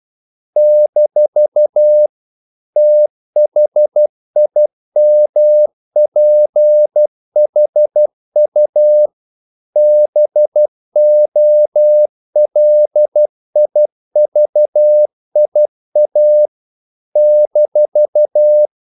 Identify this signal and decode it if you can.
Morse code, heard as a keyed tone, about 12 wpm, 600 Hz. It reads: - THIMPHU BOLIVIA -
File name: CW / Practice / NationalCapitals12